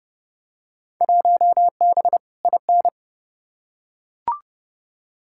Morse code, heard as a keyed tone, about 30 wpm, 700 Hz. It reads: 1B IN
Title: Field Day Exchanges - 30wpm